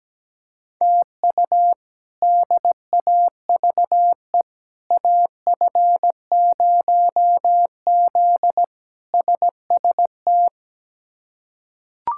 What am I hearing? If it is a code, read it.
Morse code, heard as a keyed tone, about 17 wpm, 700 Hz. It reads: TU DAVE AF0Z SST
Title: K1USN SST QSO Practice - 17wpm